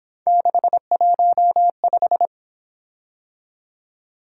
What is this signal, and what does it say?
Morse code, heard as a keyed tone, about 26 wpm, 700 Hz. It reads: 615
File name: Speed-Racing - Three Letter-Number Combinations - 22wpm